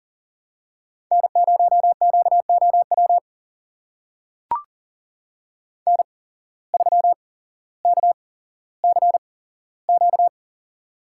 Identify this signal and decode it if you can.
Morse code, heard as a keyed tone, about 37 wpm, spaced as for 17 wpm, 700 Hz. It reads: N0QOW N3KCQ
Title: USA Call Signs - 5 Characters - 8x Character Spacing - 40wpm